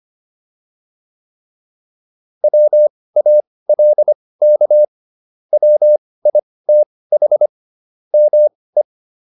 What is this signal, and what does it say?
Morse code, heard as a keyed tone, about 25 wpm, spaced as for 18 wpm, 600 Hz. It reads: WALK WITH ME